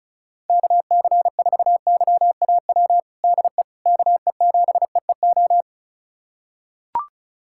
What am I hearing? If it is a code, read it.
Morse code, heard as a keyed tone, about 34 wpm, 690 Hz. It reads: KC4YAW DE KE7EEO
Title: USA Callsign DE Callsign 35wpm